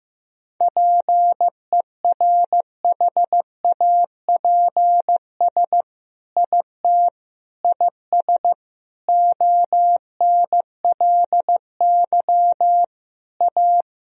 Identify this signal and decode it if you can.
Morse code, heard as a keyed tone, about 16 wpm, 700 Hz. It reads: PERHAPS IT IS ONLY A
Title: Sentences from Top 1000 Words - Review Part 2 - 15wpm